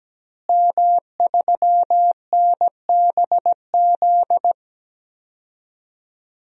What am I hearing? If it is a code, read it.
Morse code, heard as a keyed tone, about 17 wpm, 700 Hz. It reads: M3NBZ
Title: DXCC-100 Call Signs - 5 Characters - Repeated 2x - 17wpm